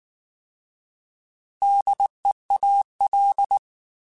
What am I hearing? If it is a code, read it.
Morse code, heard as a keyed tone, about 19 wpm, 800 Hz. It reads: DEAL